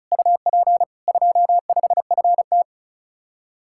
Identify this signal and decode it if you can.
Morse code, heard as a keyed tone, about 44 wpm, 700 Hz. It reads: UP 25FT
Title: Speed-Racing Common QSO Phrases - 35wpm